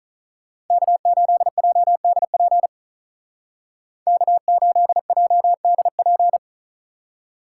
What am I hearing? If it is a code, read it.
Morse code, heard as a keyed tone, about 43 wpm, 700 Hz. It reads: K8JDP K8JDP
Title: Speed Racing - USA Call Signs 35wpm